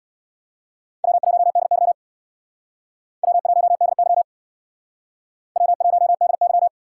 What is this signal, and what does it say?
Morse code, heard as a keyed tone, about 64 wpm, 690 Hz. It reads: W1DY W1DY W1DY